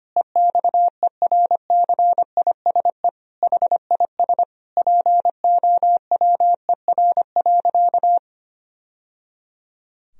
Morse code, heard as a keyed tone, about 25 wpm, 700 Hz. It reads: EXERCISE HIS POWER.
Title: Sentences from Top 1000 Words - Review Part 4 - 25wpm